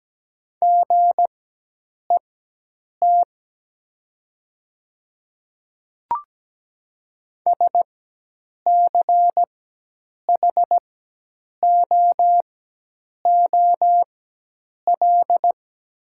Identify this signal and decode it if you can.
Morse code, heard as a keyed tone, about 17 wpm, spaced as for 8 wpm, 700 Hz. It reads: GET SCHOOL
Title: 4x Character Spacing - Top 300 Words - 17wpm